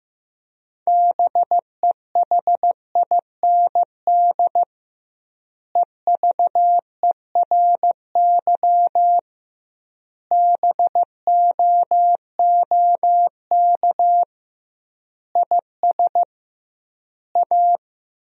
Morse code, heard as a keyed tone, about 15 wpm, 700 Hz. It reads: BEHIND EVERY BOOK IS A
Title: 2x Word-Spacing - Sentences from Top 500 Words - 15wpm